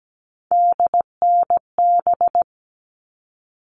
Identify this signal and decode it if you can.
Morse code, heard as a keyed tone, about 17 wpm, 700 Hz. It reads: DNB
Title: All Cut Numbers - 3-Digit 17wpm